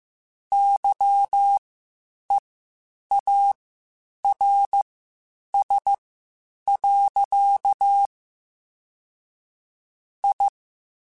Morse code, heard as a keyed tone, about 15 wpm, spaced as for 8 wpm, 800 Hz. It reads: YEARS. I
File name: QSOs Spaced-mp3s/kc7haa @13 sp